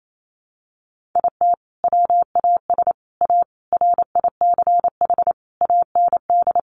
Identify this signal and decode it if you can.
Morse code, heard as a keyed tone, about 28 wpm, 700 Hz. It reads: IT WAS A RICH AND
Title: Sentences from Top 1100 Words 28wpm